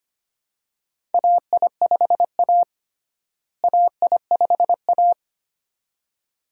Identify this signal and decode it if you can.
Morse code, heard as a keyed tone, about 25 wpm, 700 Hz. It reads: AI5A AI5A
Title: USA Call Signs - 4 Characters - Repeated 2x - 25wpm